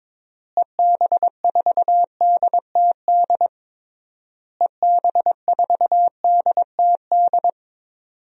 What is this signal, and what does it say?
Morse code, heard as a keyed tone, about 22 wpm, 700 Hz. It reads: EB4DTD EB4DTD
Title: DXCC-100 Call Signs - 6 Characters - Repeated 3x - 22wpm